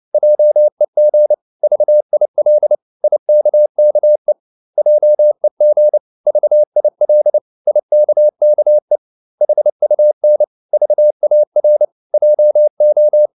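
Morse code, heard as a keyed tone, about 29 wpm, 600 Hz. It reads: JEG VIL IKKE JEG VIL IKKE HUN VAR JO
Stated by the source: Fast (QRQ) Morse at 29 wpm, from CW med Gnister.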